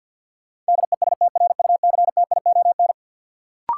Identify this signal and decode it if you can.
Morse code, heard as a keyed tone, about 46 wpm, 700 Hz. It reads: DESTRUCTION
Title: Sets of 5 Words Encore - New Word Drill - 50wpm